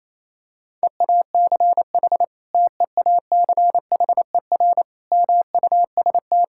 Morse code, heard as a keyed tone, about 28 wpm, 700 Hz. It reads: EACH TEACHER MUST
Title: Sentences from Top 1000 Words - Review Part 6 - 28wpm